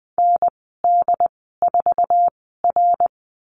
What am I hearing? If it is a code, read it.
Morse code, heard as a keyed tone, about 20 wpm, spaced as for 14 wpm, 700 Hz. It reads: ND4R
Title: 088 Lesson 15g - Number 5 - Call Signs 20wpm